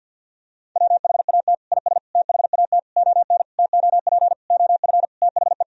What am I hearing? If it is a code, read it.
Morse code, heard as a keyed tone, about 52 wpm, 700 Hz. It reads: WHAT IS THAT ON TOP OF THE
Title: Sentences from Top 1000 Words - Review Part 2 - 50wpm